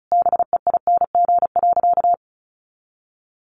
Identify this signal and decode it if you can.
Morse code, heard as a keyed tone, about 34 wpm, 700 Hz. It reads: BEING.
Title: Sentences from Top 1500 Words 35wpm